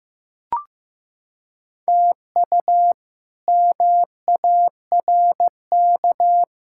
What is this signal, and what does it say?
Morse code, heard as a keyed tone, about 15 wpm, 700 Hz. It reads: TU MARK
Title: K1USN SST QSO Practice - 15wpm